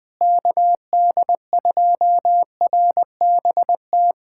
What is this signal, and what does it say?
Morse code, heard as a keyed tone, about 20 wpm, 700 Hz. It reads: KD2RBT
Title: USA Callsign DE Callsign - 6 Characters - 20wpm